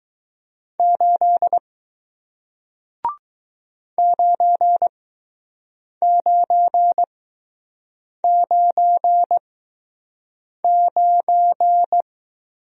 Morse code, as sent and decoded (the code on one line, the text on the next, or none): ---.. / ----. ----. ----. ----.
8 9999